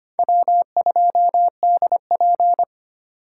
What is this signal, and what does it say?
Morse code, heard as a keyed tone, about 32 wpm, 700 Hz. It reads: W2DP